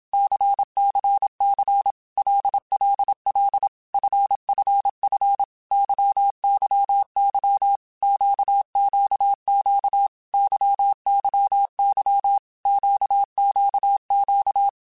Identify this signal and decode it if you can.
Morse code, heard as a keyed tone, about 27 wpm, 800 Hz. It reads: CCC LLL FFF YYY QQQ YYY QQQ
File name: Basic Letter Groups By7Groups / LFCYQ / LFCYQ@25